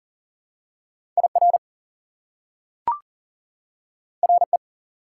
Morse code, heard as a keyed tone, about 40 wpm, 700 Hz. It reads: IR RE
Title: Prefixes and Suffixes 40wpm